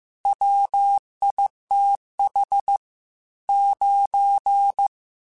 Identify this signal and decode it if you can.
Morse code, heard as a keyed tone, about 15 wpm, 800 Hz. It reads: WITH 9
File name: Technical Subjects / Technical Subjects-mp3s / AND Gate @13